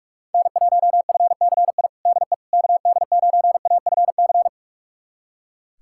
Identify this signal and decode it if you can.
Morse code, heard as a keyed tone, about 44 wpm, 700 Hz. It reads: N1FCI DE KD9AFC
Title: USA Callsign DE Callsign 45wpm